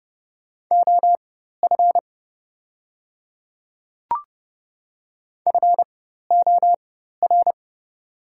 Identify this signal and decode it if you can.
Morse code, heard as a keyed tone, about 30 wpm, spaced as for 14 wpm, 700 Hz. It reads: OF FOR
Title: 4x Character Spacing - Top 100 Words - 30wpm